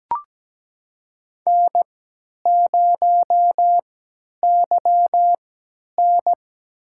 Morse code, heard as a keyed tone, about 17 wpm, spaced as for 10 wpm, 700 Hz. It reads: N0YN
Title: USA Call Signs - 4 Characters - 3x Character Spacing - 17wpm